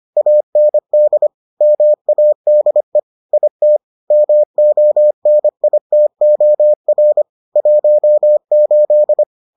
Morse code, heard as a keyed tone, about 25 wpm, 600 Hz. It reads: AND MADE IT MONITOR 18